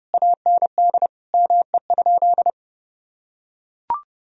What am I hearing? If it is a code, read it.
Morse code, heard as a keyed tone, about 30 wpm, 700 Hz. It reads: AND ME?
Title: Sentences from Top 1000 Words - Review Part 2 - 30wpm